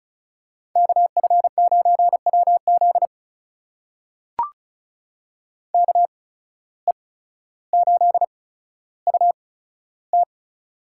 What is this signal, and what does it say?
Morse code, heard as a keyed tone, about 36 wpm, spaced as for 15 wpm, 700 Hz. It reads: KF9WZ KE8UT
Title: USA Call Signs - 5 Characters - 8x Character Spacing - 35wpm